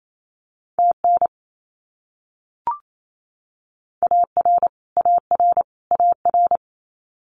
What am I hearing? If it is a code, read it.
Morse code, heard as a keyed tone, about 28 wpm, 700 Hz. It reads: TN AR AR AR
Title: US States Repeated 3x 28wpm